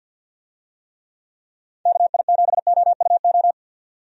Morse code, heard as a keyed tone, about 52 wpm, 700 Hz. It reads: KI7OUO
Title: USA Call Signs - 6 Characters - 50wpm